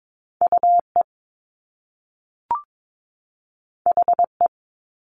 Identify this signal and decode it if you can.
Morse code, heard as a keyed tone, about 22 wpm, 700 Hz. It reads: UE HE
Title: Top Two Letter Combinations in Words 22wpm